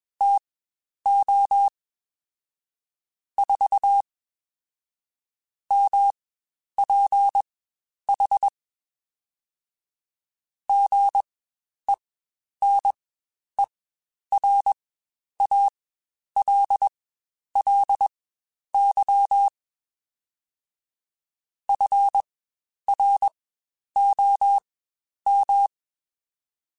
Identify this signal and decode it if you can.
Morse code, heard as a keyed tone, about 21 wpm, spaced as for 10 wpm, 800 Hz. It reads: TO 4 MPH GENERALLY FROM
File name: Spaced Characters / Spaced Characters-mp3s / Weather 4 14 sp @20